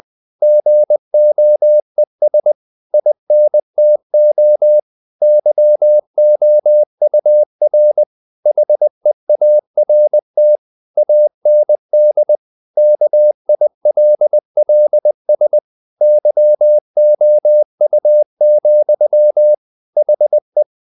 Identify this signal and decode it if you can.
Morse code, heard as a keyed tone, about 20 wpm, 600 Hz. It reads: GOES INTO YOUR HEART AND KILLS YOU, HE
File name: the old man and the sea / the old man and the sea part 1 20wpm